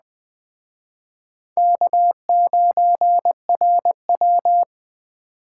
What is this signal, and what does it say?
Morse code, heard as a keyed tone, about 20 wpm, 700 Hz. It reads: K9RW
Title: Speed Racing - USA 1x2 Call Signs 17wpm